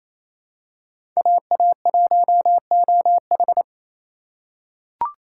Morse code, heard as a keyed tone, about 28 wpm, 700 Hz. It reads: AA1OH
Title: USA Call Signs - 5 Characters - 2x Character Spacing - 28wpm